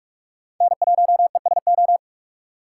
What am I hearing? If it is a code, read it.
Morse code, heard as a keyed tone, about 46 wpm, 700 Hz. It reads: N1ESO
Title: USA Call Signs - 4x Character Spacing - 45wpm